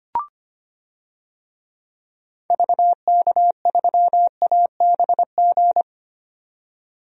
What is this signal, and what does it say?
Morse code, heard as a keyed tone, about 25 wpm, 700 Hz. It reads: VK3ABG
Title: DXCC-100 Call Signs - 6 Characters - 25wpm